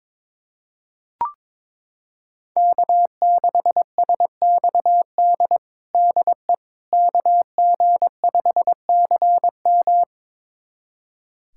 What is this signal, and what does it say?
Morse code, heard as a keyed tone, about 22 wpm, 700 Hz. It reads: K6SXD DE KG5CM
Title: USA Callsign DE Callsign - 5 Characters - 22wpm